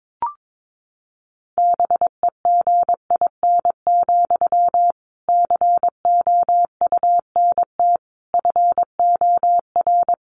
Morse code, heard as a keyed tone, about 22 wpm, 700 Hz. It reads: BEGIN, COUNT FOR
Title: Sentences from Top 1500 Words 22wpm